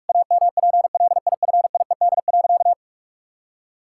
Morse code, heard as a keyed tone, about 41 wpm, 700 Hz. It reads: AMPLIFIED.
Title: Pangram Challenge v2 - 45wpm